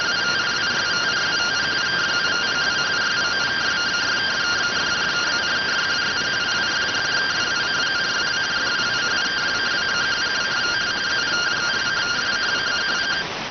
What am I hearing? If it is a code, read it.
Two-tone teleprinter signal, about 170 Hz shift.